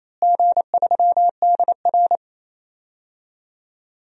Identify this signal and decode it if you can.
Morse code, heard as a keyed tone, about 28 wpm, 700 Hz. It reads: G3DR
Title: DXCC-100 Call Signs - 4 Characters - Repeated 3x - 28wpm